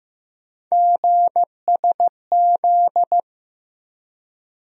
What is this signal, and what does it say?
Morse code, heard as a keyed tone, about 15 wpm, 700 Hz. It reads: GSZ
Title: USA Call Sign Postfixes - 3 Letters 15wpm